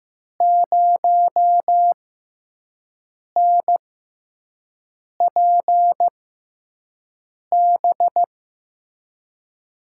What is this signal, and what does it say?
Morse code, heard as a keyed tone, about 15 wpm, spaced as for 5 wpm, 700 Hz. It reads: 0NPB
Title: USA Call Signs - 6 Characters - 6x Character Spacing - 15wpm